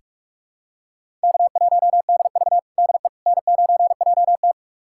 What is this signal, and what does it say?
Morse code, heard as a keyed tone, about 46 wpm, 700 Hz. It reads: K1DV DE N9JT
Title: USA Callsign DE Callsign - 4 Characters - 45wpm